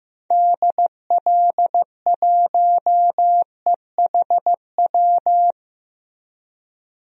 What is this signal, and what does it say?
Morse code, heard as a keyed tone, about 15 wpm, 700 Hz. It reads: DL1EHW
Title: DXCC-100 Call Signs - 6 Characters - Repeated 2x - 15wpm